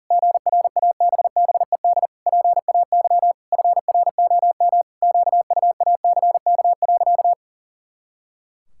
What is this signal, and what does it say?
Morse code, heard as a keyed tone, about 40 wpm, 700 Hz. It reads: GRABBED PAY FROM QUACK.
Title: Pangram Challenge v2 - 40wpm